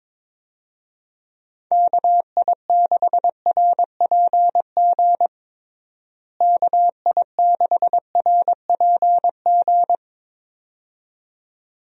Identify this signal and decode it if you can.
Morse code, heard as a keyed tone, about 22 wpm, 700 Hz. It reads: KI6RPG KI6RPG
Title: USA Call Signs - 6 Characters - Repeated 2x - 22wpm